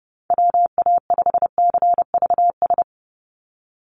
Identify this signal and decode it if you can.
Morse code, heard as a keyed tone, about 30 wpm, 700 Hz. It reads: WA5CVS
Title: USA Call Signs 30wpm